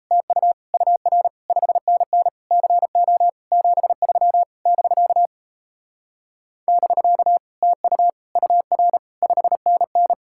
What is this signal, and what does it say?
Morse code, heard as a keyed tone, about 38 wpm, 700 Hz. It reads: TU UR 5NN CO 73 <BK> <BK> TU UR 5NN